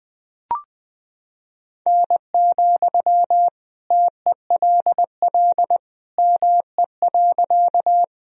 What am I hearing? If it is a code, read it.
Morse code, heard as a keyed tone, about 21 wpm, 700 Hz. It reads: N, TELL ME.
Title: Sentences from Top 200 Words and Abbreviations 20wpm